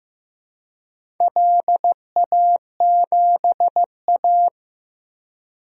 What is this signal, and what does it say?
Morse code, heard as a keyed tone, about 15 wpm, 700 Hz. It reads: LA7A
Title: DXCC-100 Call Signs - 4 Characters - Repeated 3x - 15wpm